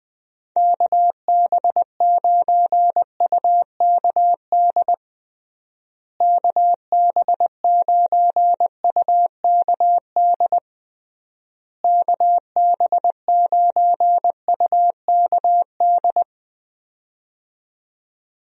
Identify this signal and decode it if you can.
Morse code, heard as a keyed tone, about 20 wpm, 700 Hz. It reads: KB9UKD KB9UKD KB9UKD
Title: USA Call Signs - 6 Characters - Repeated 3x - 20wpm